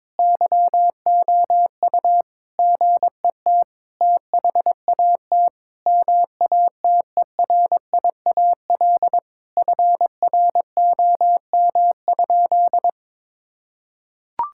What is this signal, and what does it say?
Morse code, heard as a keyed tone, about 22 wpm, 700 Hz. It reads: YOU GET THAT MATERIAL FROM?
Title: Sentences from Top 1000 Words - Review Part 5 - 22wpm